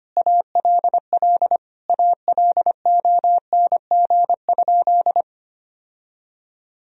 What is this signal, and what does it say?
Morse code, heard as a keyed tone, about 25 wpm, 700 Hz. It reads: ALL ALONG?
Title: Sentences from Top 1000 Words - Review Part 3 - 25wpm